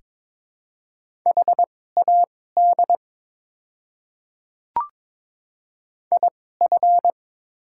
Morse code, heard as a keyed tone, about 22 wpm, spaced as for 16 wpm, 700 Hz. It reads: HAD IF
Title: 2x Character Spacing - Top 300 Words - 22wpm